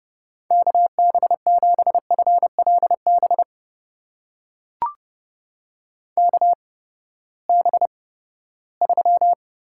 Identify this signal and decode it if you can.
Morse code, heard as a keyed tone, about 30 wpm, spaced as for 17 wpm, 700 Hz. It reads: KB7FLB KB3
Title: USA Call Signs - 6 Characters - 8x Character Spacing - 30wpm